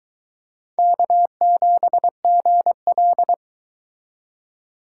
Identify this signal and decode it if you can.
Morse code, heard as a keyed tone, about 23 wpm, 700 Hz. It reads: K7GL